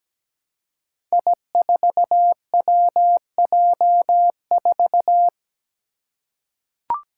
Morse code, heard as a keyed tone, about 17 wpm, 700 Hz. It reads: I4WJ4
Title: Five Challenging Letter-Number Combinations 17wpm